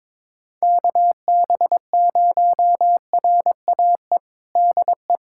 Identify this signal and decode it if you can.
Morse code, heard as a keyed tone, about 22 wpm, 700 Hz. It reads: KB0RAE DE
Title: USA Callsign DE Callsign 22wpm